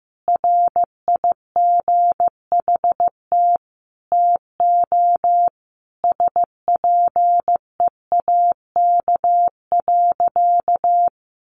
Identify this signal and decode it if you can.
Morse code, heard as a keyed tone, about 15 wpm, 700 Hz. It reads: RIGHT TO SPEAK.
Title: Sentences from Top 1300 Words 15wpm